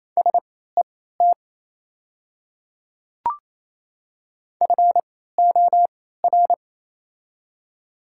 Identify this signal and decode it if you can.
Morse code, heard as a keyed tone, about 28 wpm, spaced as for 16 wpm, 700 Hz. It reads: SET FOR
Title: 3x Character Spacing - Top 200 Words - 28wpm